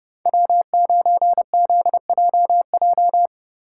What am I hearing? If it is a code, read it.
Morse code, heard as a keyed tone, about 30 wpm, 700 Hz. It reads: W9ZJJ
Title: USA Call Signs - 5 Characters - 3x Character Spacing - 30wpm